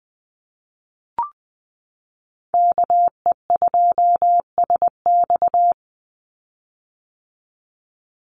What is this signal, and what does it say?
Morse code, heard as a keyed tone, about 20 wpm, 700 Hz. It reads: KE2SX
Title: Call Signs - 20wpm